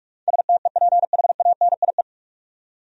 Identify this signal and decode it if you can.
Morse code, heard as a keyed tone, about 46 wpm, 700 Hz. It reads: STEPHANIE
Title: USA Common Names - 45wpm